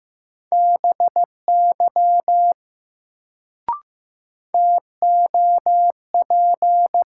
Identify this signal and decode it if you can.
Morse code, heard as a keyed tone, about 15 wpm, 700 Hz. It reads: BY TOP
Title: Sets of 2 Words - New Word Drill - 15wpm